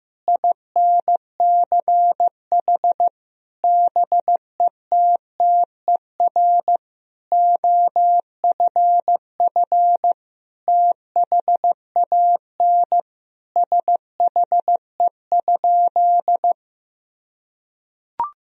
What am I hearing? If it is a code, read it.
Morse code, heard as a keyed tone, about 15 wpm, 700 Hz. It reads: INCH BETTER OFF THAN SHE?
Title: Sentences from Top 1000 Words - Review Part 3 - 15wpm